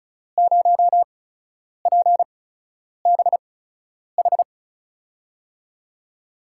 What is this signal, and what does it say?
Morse code, heard as a keyed tone, about 36 wpm, spaced as for 10 wpm, 700 Hz. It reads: 0PBH
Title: USA Call Signs - 6 Characters - 8x Character Spacing - 35wpm